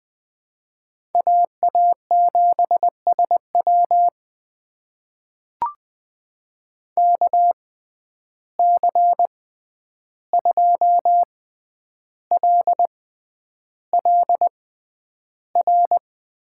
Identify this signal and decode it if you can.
Morse code, heard as a keyed tone, about 20 wpm, spaced as for 10 wpm, 700 Hz. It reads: AA7SW KC2LLR
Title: USA Call Signs - 6x Character Spacing - 20wpm